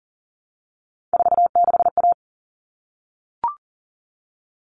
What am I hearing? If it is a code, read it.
Morse code, heard as a keyed tone, about 40 wpm, 700 Hz. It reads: 46A
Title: All Cut Numbers - 3-Digit 40wpm